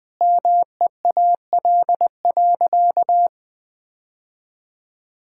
Morse code, heard as a keyed tone, about 21 wpm, 700 Hz. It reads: MEAL.
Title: All-Age Appropriate Jokes - 20wpm